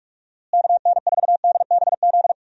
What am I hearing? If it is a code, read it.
Morse code, heard as a keyed tone, about 41 wpm, 700 Hz. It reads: KN4DBZ
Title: USA Call Signs - 6 Characters - Repeated 2x - 45wpm